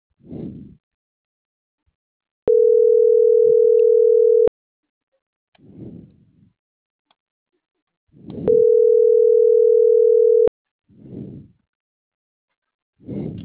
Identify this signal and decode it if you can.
Ringback tone (North America), 440 + 480 Hz.